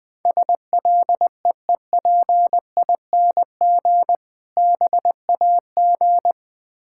Morse code, heard as a keyed tone, about 20 wpm, 690 Hz. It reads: SLEEPING BAG
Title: Difficult Words with Hints - 20wpm